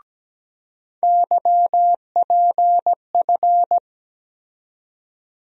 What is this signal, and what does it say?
Morse code, heard as a keyed tone, about 17 wpm, 700 Hz. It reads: YPF
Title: USA Call Sign Postfixes - 3 Letters 17wpm